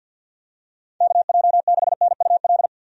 Morse code, heard as a keyed tone, about 49 wpm, 700 Hz. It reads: KJ6NUL
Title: USA Call Signs - 6 Characters - 50wpm